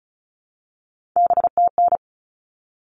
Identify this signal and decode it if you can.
Morse code, heard as a keyed tone, about 34 wpm, 700 Hz. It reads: BTN
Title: All Cut Numbers - 3-Digit 35wpm